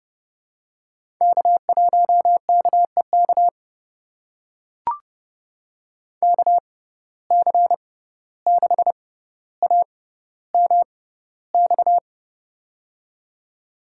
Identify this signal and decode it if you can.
Morse code, heard as a keyed tone, about 32 wpm, spaced as for 15 wpm, 700 Hz. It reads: K1KEK KC6AMX
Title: USA Call Signs - 6x Character Spacing - 30wpm